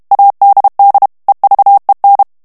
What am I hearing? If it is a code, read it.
Morse code, heard as a keyed tone, about 32 wpm, 800 Hz. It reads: ADD EVEN
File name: morse mp3 / ANGLAIS texte clair 1 vitesse 30 mots minute